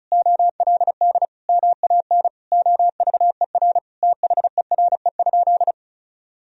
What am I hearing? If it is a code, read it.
Morse code, heard as a keyed tone, about 36 wpm, 700 Hz. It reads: OLD MAN OVER THERE?
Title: Sentences from Top 200 Words and Abbreviations 35wpm